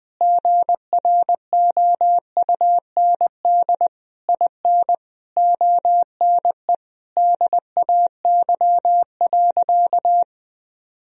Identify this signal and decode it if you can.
Morse code, heard as a keyed tone, about 20 wpm, 700 Hz. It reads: GROUND IN ONE DAY.
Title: Sentences from Top 1000 Words - Review Part 3 - 20wpm